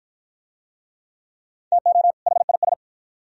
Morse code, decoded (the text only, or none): TO HIS